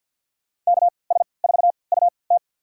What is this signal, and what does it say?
Morse code, heard as a keyed tone, about 52 wpm, spaced as for 29 wpm, 700 Hz. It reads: KS4UT
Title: USA Call Signs - 5 Characters - 3x Character Spacing - 50wpm